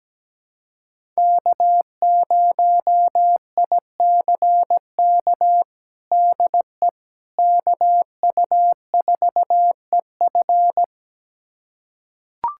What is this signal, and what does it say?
Morse code, heard as a keyed tone, about 17 wpm, 700 Hz. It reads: K0ICK DE KU4EF
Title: USA Callsign DE Callsign - 5 Characters - 17wpm